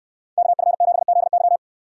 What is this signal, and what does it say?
Morse code, heard as a keyed tone, about 64 wpm, 700 Hz. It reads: KF7ZY